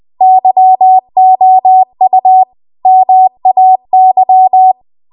Morse code, heard as a keyed tone, about 20 wpm, 750 Hz. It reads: YOU MAY